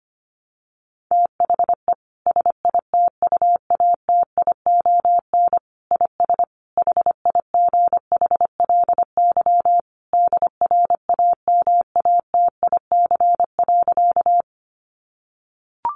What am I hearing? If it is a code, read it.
Morse code, heard as a keyed tone, about 25 wpm, 700 Hz. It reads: THE SITUATION IS HIGHLY DRAMATIC.
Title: Sentences from Top 1300 Words 25wpm